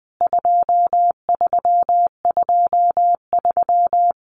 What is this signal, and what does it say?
Morse code, heard as a keyed tone, about 19 wpm, 700 Hz. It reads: 2323
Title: CWT Exchange 20wpm